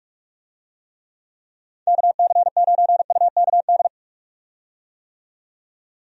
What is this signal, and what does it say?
Morse code, heard as a keyed tone, about 46 wpm, 700 Hz. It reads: KC9UKD
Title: USA Call Signs - 6 Characters - 45wpm